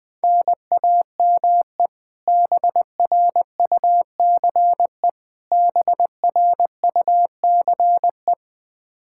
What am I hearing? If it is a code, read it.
Morse code, heard as a keyed tone, about 20 wpm, 700 Hz. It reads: NAME BRUCE BRUCE
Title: Practice QSO - Exchange 1 - Beginning - 20wpm